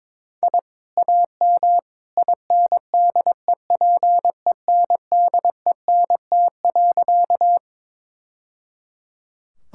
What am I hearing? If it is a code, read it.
Morse code, heard as a keyed tone, about 22 wpm, 700 Hz. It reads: I AM INDEPENDENT.
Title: Sentences from Top 1000 Words - Review Part 6 - 22wpm